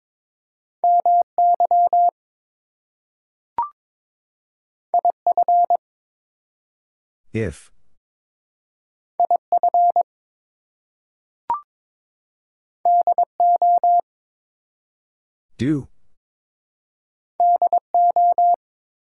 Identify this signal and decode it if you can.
Morse code, heard as a keyed tone, about 22 wpm, 700 Hz. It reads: MY IF IF DO DO
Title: Two Letter Words - 22wpm